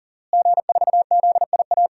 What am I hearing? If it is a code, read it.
Morse code, heard as a keyed tone, about 37 wpm, 700 Hz. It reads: G4ZIA
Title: DXCC-100 Call Signs - 5 Characters - Repeated 3x - 40wpm